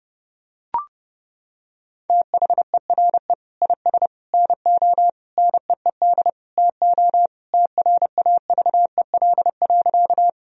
Morse code, heard as a keyed tone, about 32 wpm, 700 Hz. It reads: THERE IS NO NEED TO TRAVEL.
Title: Sentences from Top 1000 Words - Review Part 2 - 30wpm